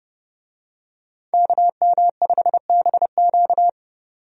Morse code, heard as a keyed tone, about 30 wpm, 700 Hz. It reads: KM5BQ
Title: USA Call Signs - 5 Characters - 30wpm